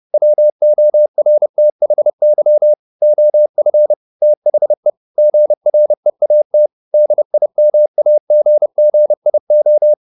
Morse code, heard as a keyed tone, about 30 wpm, 600 Hz. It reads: WORTHY OF THE GREAT DIMAGGIO
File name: the old man and the sea / the old man and the sea 1 30wpm